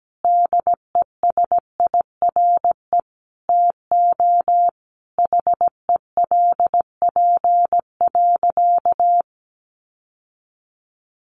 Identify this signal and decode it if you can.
Morse code, heard as a keyed tone, about 17 wpm, 700 Hz. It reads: DESIRE TO HELP.
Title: Sentences from Top 1100 Words 17wpm